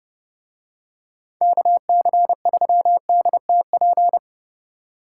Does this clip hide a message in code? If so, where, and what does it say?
Yes: Morse code from 1.4 to 4.2 s, heard as a keyed tone, about 30 wpm, 700 Hz. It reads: KC3DTP